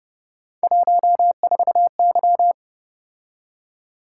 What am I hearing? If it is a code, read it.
Morse code, heard as a keyed tone, about 32 wpm, 700 Hz. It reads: 14Y